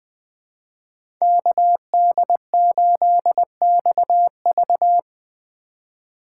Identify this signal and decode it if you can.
Morse code, heard as a keyed tone, about 20 wpm, 700 Hz. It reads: KD8XV
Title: USA Call Signs - 5 Characters - 4x Character Spacing - 20wpm